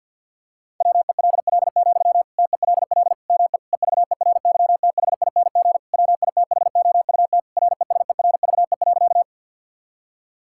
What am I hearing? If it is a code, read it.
Morse code, heard as a keyed tone, about 50 wpm, 700 Hz. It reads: WELL, TELL ME EVERYTHING WITHOUT RESERVE.
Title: Sentences from Top 1000 Words - Review Part 6 - 50wpm